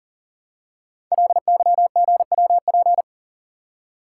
Morse code, heard as a keyed tone, about 39 wpm, 700 Hz. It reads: LYGWP
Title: Code Groups - 5 Letter:Numbers - 40wpm